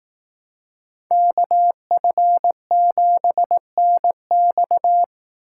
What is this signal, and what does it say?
Morse code, heard as a keyed tone, about 18 wpm, 700 Hz. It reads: KF7NX
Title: Speed Racing - USA Call Signs 15wpm